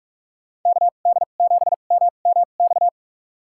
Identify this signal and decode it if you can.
Morse code, heard as a keyed tone, about 46 wpm, spaced as for 32 wpm, 700 Hz. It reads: KD7MMX
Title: USA Call Signs - 2x Character Spacing - 45wpm